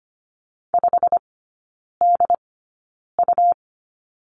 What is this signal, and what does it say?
Morse code, heard as a keyed tone, about 25 wpm, spaced as for 9 wpm, 700 Hz. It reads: 5DU
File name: Lesson 17e - Letter u - Three Characters 25wpm